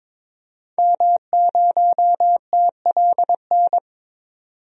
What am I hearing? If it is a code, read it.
Morse code, heard as a keyed tone, about 22 wpm, 700 Hz. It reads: M0TLN
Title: DXCC-100 Call Signs - 5 Characters - 22wpm